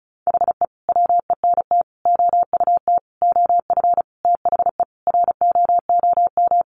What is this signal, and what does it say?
Morse code, heard as a keyed tone, about 34 wpm, 700 Hz. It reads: HE WENT OUT OF THE ROOM
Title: Sentences from Top 1100 Words 35wpm